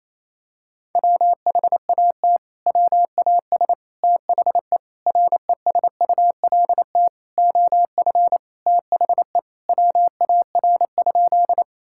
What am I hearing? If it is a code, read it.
Morse code, heard as a keyed tone, about 28 wpm, 700 Hz. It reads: WHAT WAS THE RESULT OF THE WAR?
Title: Sentences from Top 1000 Words - Review Part 5 - 28wpm